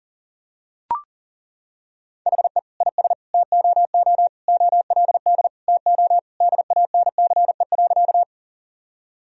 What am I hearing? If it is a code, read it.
Morse code, heard as a keyed tone, about 37 wpm, 700 Hz. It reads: HE IS TOO OLD TO DANCE.
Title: Sentences from Top 1000 Words - Review Part 4 - 40wpm